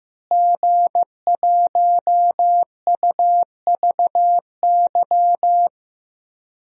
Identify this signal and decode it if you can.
Morse code, heard as a keyed tone, about 15 wpm, 700 Hz. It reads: G1UVY
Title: DXCC-100 Call Signs - 5 Characters - Repeated 2x - 15wpm